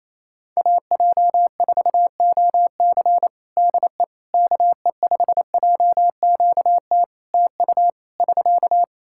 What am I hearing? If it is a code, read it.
Morse code, heard as a keyed tone, about 28 wpm, 700 Hz. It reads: AJ4OC DE KE5JQT TU <SK>
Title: K1USN SST QSO Practice 28wpm